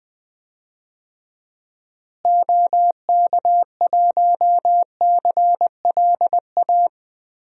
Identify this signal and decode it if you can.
Morse code, heard as a keyed tone, about 20 wpm, 700 Hz. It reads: OK1CLA
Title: DXCC-100 Call Signs - 6 Characters - Repeated 3x - 20wpm